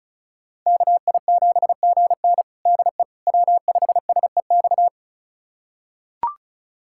Morse code, heard as a keyed tone, about 36 wpm, 700 Hz. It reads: KI7GN DE W5SEX
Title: USA Callsign DE Callsign - 5 Characters - 35wpm